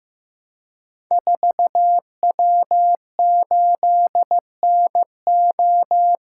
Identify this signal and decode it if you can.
Morse code, heard as a keyed tone, about 15 wpm, 700 Hz. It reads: 4W8NO
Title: Code Groups - 5 Letter:Numbers - 15wpm